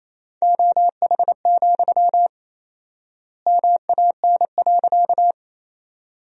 Morse code, heard as a keyed tone, about 27 wpm, 700 Hz. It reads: OH, MAN.
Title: 4x Word-Spacing - Sentences from Top 500 Words - 28wpm